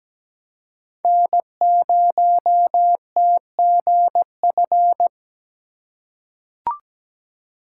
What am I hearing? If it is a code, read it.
Morse code, heard as a keyed tone, about 17 wpm, 700 Hz. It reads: N0TGF
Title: USA Call Signs - 4x Character Spacing - 17wpm